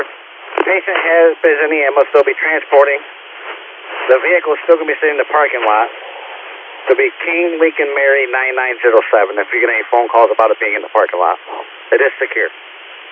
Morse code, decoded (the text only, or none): none